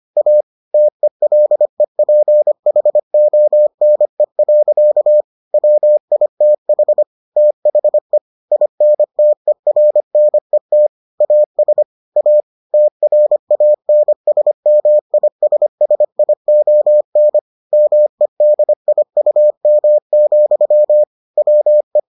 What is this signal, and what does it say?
Morse code, heard as a keyed tone, about 25 wpm, 600 Hz. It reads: A TELEPHONE. WITH THE INTERNET AS A TRANSMISSION MEDIUM, WE